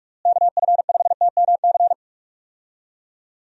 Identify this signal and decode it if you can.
Morse code, heard as a keyed tone, about 46 wpm, 700 Hz. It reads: KF5TMC
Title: USA Call Signs - 6 Characters - Repeated 3x - 45wpm